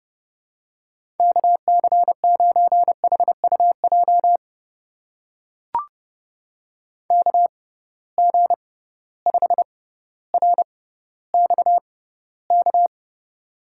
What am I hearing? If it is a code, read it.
Morse code, heard as a keyed tone, about 30 wpm, spaced as for 15 wpm, 700 Hz. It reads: KC9HUJ KG5RXK
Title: USA Call Signs - 6x Character Spacing - 30wpm